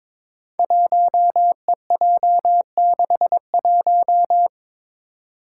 Morse code, heard as a keyed tone, about 22 wpm, 700 Hz. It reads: 1EJ61
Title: Five Challenging Letter-Number Combinations 22wpm